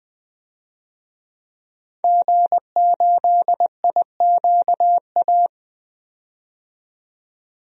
Morse code, heard as a keyed tone, about 20 wpm, 700 Hz. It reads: G8IQA
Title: DXCC-100 Call Signs - 5 Characters - 20wpm